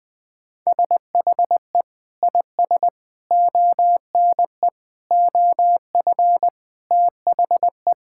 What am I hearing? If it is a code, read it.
Morse code, heard as a keyed tone, about 20 wpm, 700 Hz. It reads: SHE IS ONE OF THE
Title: Sentences with 6-Word Phrases - 20wpm